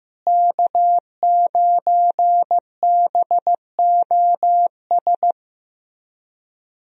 Morse code, heard as a keyed tone, about 15 wpm, 700 Hz. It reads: K9BOS
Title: USA Call Signs - 5 Characters - Repeated 3x - 15wpm